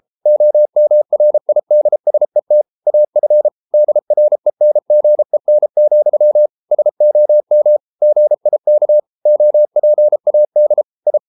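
Morse code, heard as a keyed tone, about 33 wpm, 600 Hz. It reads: OMRIDSET AF DRENGEN, SOM GIK OPAD I
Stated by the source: Morse code at 33 wpm, QRQ.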